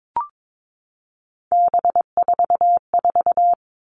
Morse code, heard as a keyed tone, about 22 wpm, 700 Hz. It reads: B44
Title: All Cut Numbers - 3-Digit 22wpm